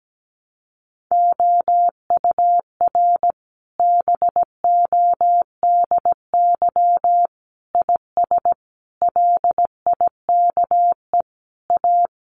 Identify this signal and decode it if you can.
Morse code, heard as a keyed tone, about 17 wpm, 700 Hz. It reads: OUR BODY IS LIKE A
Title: Sentences from Top 1000 Words 17wpm